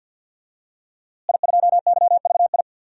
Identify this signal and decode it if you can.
Morse code, heard as a keyed tone, about 47 wpm, 700 Hz. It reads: I2YVI
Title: DXCC-100 Call Signs - 5 Characters - Repeated 3x - 50wpm